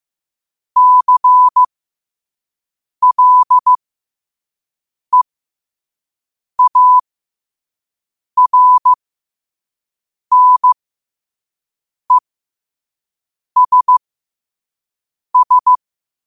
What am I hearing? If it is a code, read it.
Morse code, heard as a keyed tone, about 15 wpm, spaced as for 5 wpm, 1000 Hz. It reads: CLEARNESS